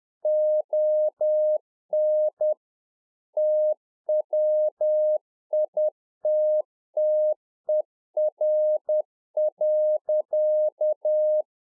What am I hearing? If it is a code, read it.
Morse code, heard as a keyed tone, about 10 wpm, 610 Hz. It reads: ON TWITTER.